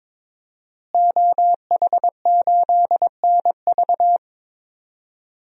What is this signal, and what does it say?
Morse code, heard as a keyed tone, about 22 wpm, 700 Hz. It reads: OH8NV
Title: DXCC-100 Call Signs - 5 Characters - Repeated 3x - 22wpm